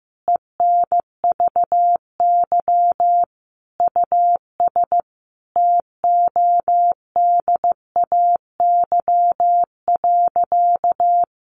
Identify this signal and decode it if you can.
Morse code, heard as a keyed tone, about 15 wpm, 700 Hz. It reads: ENVY US TODAY.
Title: A Sound Of Thunder 15wpm